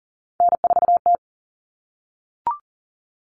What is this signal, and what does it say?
Morse code, heard as a keyed tone, about 40 wpm, 700 Hz. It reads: N4T
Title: All Cut Numbers - 3-Digit 40wpm